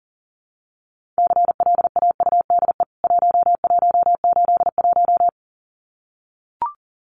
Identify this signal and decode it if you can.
Morse code, heard as a keyed tone, about 40 wpm, 700 Hz. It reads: CLAUDE 1181
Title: CWT Exchange 40wpm